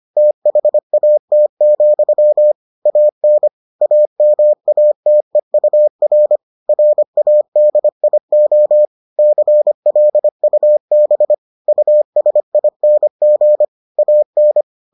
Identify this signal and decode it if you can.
Morse code, heard as a keyed tone, about 25 wpm, 600 Hz. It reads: THAT, AN AMATEUR RADIO CLUB USING AN